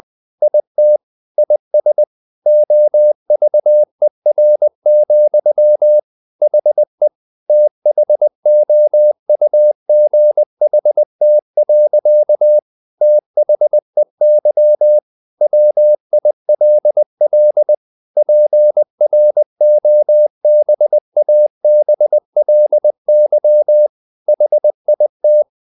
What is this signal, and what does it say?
Morse code, heard as a keyed tone, about 20 wpm, 600 Hz. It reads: IT IS OVER, HE THOUGHT. THEY WILL PROBABLY HIT
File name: the old man and the sea part 2 20wpm